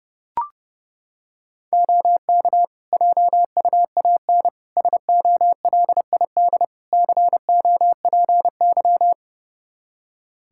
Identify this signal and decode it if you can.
Morse code, heard as a keyed tone, about 30 wpm, 700 Hz. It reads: OK JUAN SOLID COPY
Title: Practice QSO - Exchange 3 - More Info - 30wpm